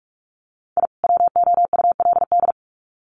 Morse code, heard as a keyed tone, about 44 wpm, 700 Hz. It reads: I WOULD
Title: Sets of 2 Words 45wpm